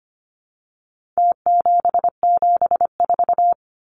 Morse code, heard as a keyed tone, about 25 wpm, 700 Hz. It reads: T774
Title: Numbers - Contest Style - 4-Digit 25wpm